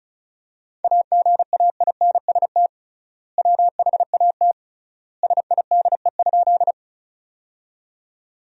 Morse code, heard as a keyed tone, about 36 wpm, 700 Hz. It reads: AGAINST WHAT SIDE?
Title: 3x Word-Spacing - Sentences from Top 400 Words - 35wpm